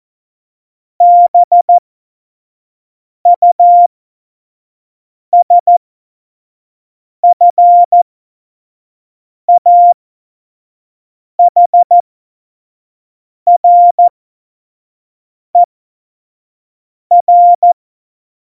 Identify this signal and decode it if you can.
Morse code, heard as a keyed tone, about 14 wpm, spaced as for 5 wpm, 700 Hz. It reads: BUSFAHRER
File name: Morsekurs / Witze / Witz 10. Oktober 17, Tempo 25